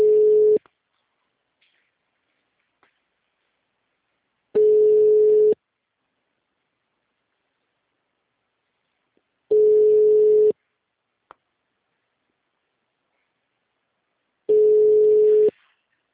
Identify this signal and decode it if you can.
Ringback tone (Europe, 425 Hz).
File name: out- -2027- - - .8